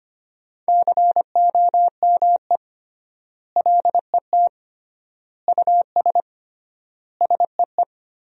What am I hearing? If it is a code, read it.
Morse code, heard as a keyed tone, about 25 wpm, 700 Hz. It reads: COME LET US SEE
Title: 3x Word-Spacing - Sentences from Top 300 Words - 25wpm